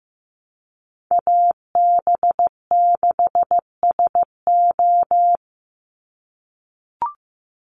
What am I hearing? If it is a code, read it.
Morse code, heard as a keyed tone, about 15 wpm, 700 Hz. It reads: AB6SO
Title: NA SOTA Chaser Callsigns 15wpm